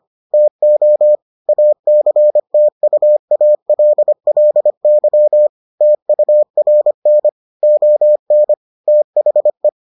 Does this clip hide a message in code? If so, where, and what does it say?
Yes: Morse code from 0.3 to 9.7 s, heard as a keyed tone, about 25 wpm, 600 Hz. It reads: TO ACTUALLY TURN ON THE